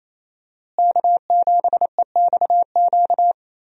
Morse code, heard as a keyed tone, about 28 wpm, 700 Hz. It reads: K7EXQ
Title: USA Call Signs - 5 Characters - Repeated 3x - 28wpm